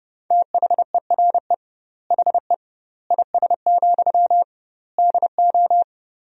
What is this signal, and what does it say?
Morse code, heard as a keyed tone, about 30 wpm, 700 Hz. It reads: THERE HE IS, DO
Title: 2x Word-Spacing - Sentences from Top 100 Words - 30wpm